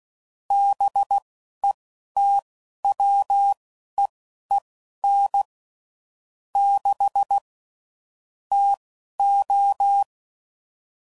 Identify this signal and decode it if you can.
Morse code, heard as a keyed tone, about 16 wpm, spaced as for 11 wpm, 800 Hz. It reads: BETWEEN 6 TO
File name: CW Code Groups @10WPM / Weather Sets / Weather 4 24